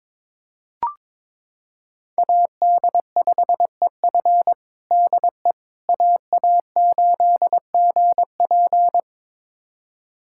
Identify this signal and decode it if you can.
Morse code, heard as a keyed tone, about 22 wpm, 700 Hz. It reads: AD5EF DE AA8GP
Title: USA Callsign DE Callsign - 5 Characters - 22wpm